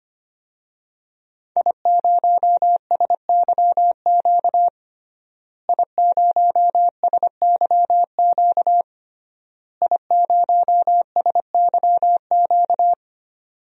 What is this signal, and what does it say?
Morse code, heard as a keyed tone, about 25 wpm, 700 Hz. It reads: I0SYQ I0SYQ I0SYQ
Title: DXCC-100 Call Signs - 5 Characters - Repeated 3x - 25wpm